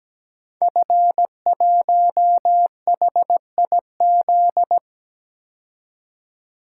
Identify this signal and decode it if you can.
Morse code, heard as a keyed tone, about 17 wpm, 700 Hz. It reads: F1HIZ
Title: Code Groups - 5 Letter:Numbers - 17wpm